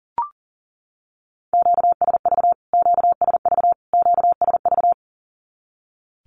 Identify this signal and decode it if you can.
Morse code, heard as a keyed tone, about 40 wpm, 700 Hz. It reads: QSV QSV QSV
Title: Q-Codes Repeated 3x 40wpm